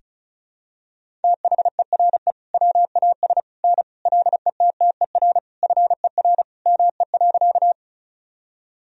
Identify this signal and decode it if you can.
Morse code, heard as a keyed tone, about 36 wpm, 700 Hz. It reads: THERE WAS N LETTER FER ME.
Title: Sentences from Top 300 Words and Abbreviations 35wpm